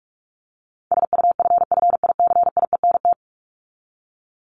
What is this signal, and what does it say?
Morse code, heard as a keyed tone, about 44 wpm, 700 Hz. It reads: SUFFICIENT
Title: LY Suffix - Top 100 Words with Base Words 45wpm